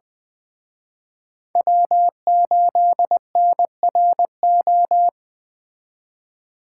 Morse code, heard as a keyed tone, about 20 wpm, 700 Hz. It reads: W8NRO